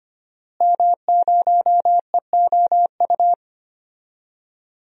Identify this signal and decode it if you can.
Morse code, heard as a keyed tone, about 25 wpm, 700 Hz. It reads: M0EOU
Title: DXCC-100 Call Signs - 5 Characters - Repeated 3x - 25wpm